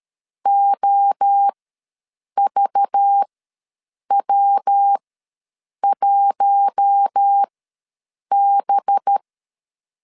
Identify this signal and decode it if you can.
Morse code, heard as a keyed tone, about 13 wpm, spaced as for 7 wpm, 800 Hz. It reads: OVW1B
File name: LEZIONI / 5 LBVC / 5 LBV ws 12wpm 3015